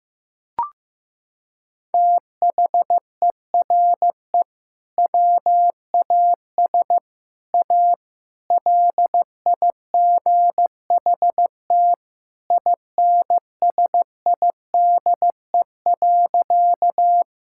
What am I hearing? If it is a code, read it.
Morse code, heard as a keyed tone, about 15 wpm, 700 Hz. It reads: THERE WAS A LIGHT INSIDE.
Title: Sentences from Top 1000 Words - Review Part 6 - 15wpm